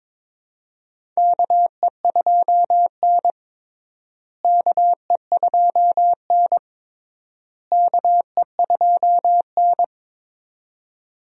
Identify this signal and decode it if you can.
Morse code, heard as a keyed tone, about 22 wpm, 700 Hz. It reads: KE2N KE2N KE2N
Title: USA Call Signs - 4 Characters - Repeated 3x - 22wpm